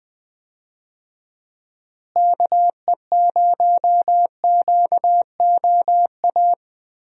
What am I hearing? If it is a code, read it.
Morse code, heard as a keyed tone, about 21 wpm, 700 Hz. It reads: KE0QOA
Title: USA Call Signs - 6 Characters - Repeated 3x - 20wpm